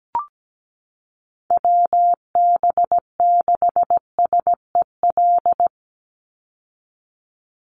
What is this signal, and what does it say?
Morse code, heard as a keyed tone, about 17 wpm, 700 Hz. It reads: WB6SEL
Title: Call Signs Repeated 3x - 17wpm